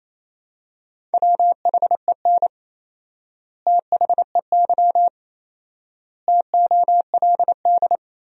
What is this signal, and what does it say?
Morse code, heard as a keyed tone, about 28 wpm, 700 Hz. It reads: WHEN THEY TOLD
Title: 4x Word-Spacing - Sentences from Top 400 Words - 28wpm